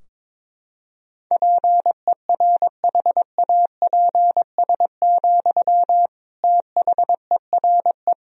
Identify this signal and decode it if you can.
Morse code, heard as a keyed tone, about 22 wpm, 700 Hz. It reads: PERHAPS, THERE
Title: Sentences from Top 1000 Words - Review Part 4 - 22wpm